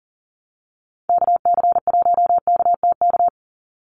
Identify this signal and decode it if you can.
Morse code, heard as a keyed tone, about 40 wpm, 700 Hz. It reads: KC1KTK
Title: USA Call Signs 40wpm